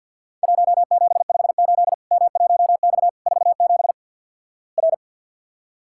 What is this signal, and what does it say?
Morse code, heard as a keyed tone, about 50 wpm, 700 Hz. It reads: 1758 M1X 47 R
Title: Worked All Europe DX Contest - QTC Exchange with Repeats - 50wpm